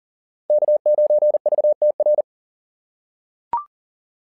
Morse code, heard as a keyed tone, about 37 wpm, 660 Hz. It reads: K9VTR
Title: SOTA QSO Practice 40wpm